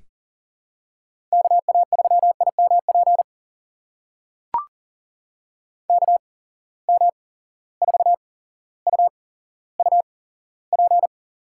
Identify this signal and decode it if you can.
Morse code, heard as a keyed tone, about 40 wpm, spaced as for 17 wpm, 700 Hz. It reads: KA3IMP KM4UUP
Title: USA Call Signs - 6 Characters - 8x Character Spacing - 40wpm